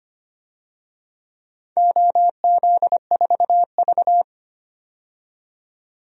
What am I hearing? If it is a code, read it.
Morse code, heard as a keyed tone, about 25 wpm, 700 Hz. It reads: OZ4V
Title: DXCC-100 Call Signs - 4 Characters - 25wpm